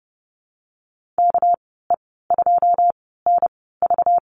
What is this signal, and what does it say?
Morse code, heard as a keyed tone, about 30 wpm, spaced as for 17 wpm, 700 Hz. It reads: KE2NV
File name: Lesson 37g - Letter x - Call Signs 30wpm